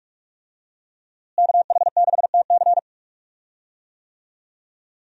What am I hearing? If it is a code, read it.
Morse code, heard as a keyed tone, about 45 wpm, 700 Hz. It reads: KH6TC
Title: DXCC-100 Call Signs - 5 Characters - Repeated 2x - 45wpm